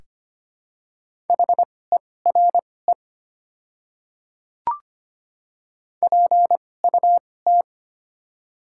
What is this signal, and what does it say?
Morse code, heard as a keyed tone, about 25 wpm, spaced as for 18 wpm, 690 Hz. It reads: HERE PUT
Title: 2x Character Spacing - Top 200 Words - 25wpm